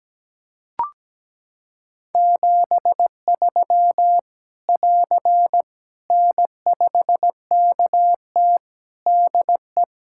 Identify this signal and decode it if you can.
Morse code, heard as a keyed tone, about 17 wpm, 700 Hz. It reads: 73 <AR> N5KT DE
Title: Practice QSO - Exchange 4 - Ending - 17wpm